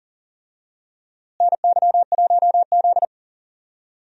Morse code, heard as a keyed tone, about 40 wpm, 700 Hz. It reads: NY1Z
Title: USA Call Signs - 4 Characters - Repeated 3x - 40wpm